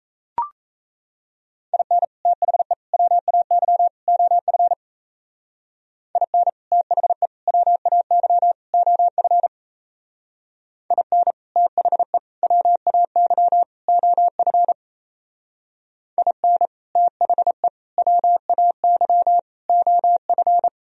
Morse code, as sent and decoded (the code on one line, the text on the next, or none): .. -. / - .... / .-- .- -.-- / --- ..-. / .. -. / - .... / .-- .- -.-- / --- ..-. / .. -. / - .... / .-- .- -.-- / --- ..-. / .. -. / - .... / .-- .- -.-- / --- ..-.
IN THE WAY OF IN THE WAY OF IN THE WAY OF IN THE WAY OF